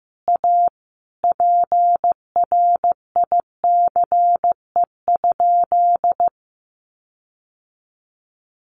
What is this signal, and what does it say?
Morse code, heard as a keyed tone, about 15 wpm, 700 Hz. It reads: A PRICE?
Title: Sentences from Top 1400 Words 15wpm